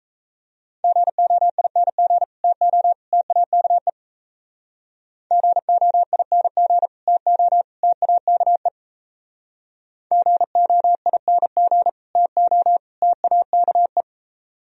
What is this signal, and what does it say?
Morse code, heard as a keyed tone, about 37 wpm, 700 Hz. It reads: GOING TO TAKE GOING TO TAKE GOING TO TAKE